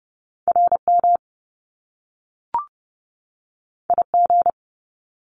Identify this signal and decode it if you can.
Morse code, heard as a keyed tone, about 30 wpm, 700 Hz. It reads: RM IG
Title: Top Two Letter Combinations in Words 30wpm